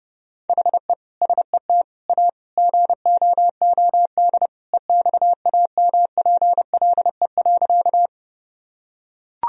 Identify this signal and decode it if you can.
Morse code, heard as a keyed tone, about 30 wpm, 700 Hz. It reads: HE SET A GOOD EXAMPLE.
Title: Sentences from Top 1000 Words - Review Part 1 - 30wpm